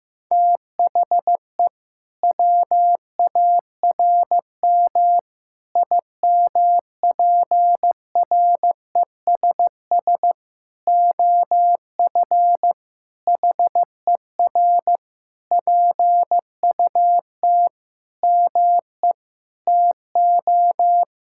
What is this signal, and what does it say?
Morse code, heard as a keyed tone, about 15 wpm, 700 Hz. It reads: THE WARM IMPRESS OF HER PUT ME TO
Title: Sentences from Top 1600 Words 15wpm